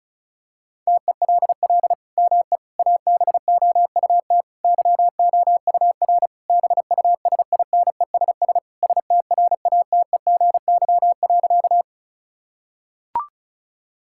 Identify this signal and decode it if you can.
Morse code, heard as a keyed tone, about 36 wpm, 700 Hz. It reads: TELL ME ABOUT YOUR BUSINESS STRATEGY.
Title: Sentences from Top 1000 Words - Review Part 5 - 35wpm